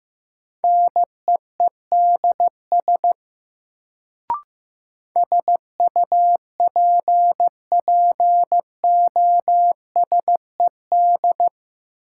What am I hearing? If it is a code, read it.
Morse code, heard as a keyed tone, about 15 wpm, 700 Hz. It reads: NEEDS SUPPOSED
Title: Sets of 3 Words - New Word Drill - 15wpm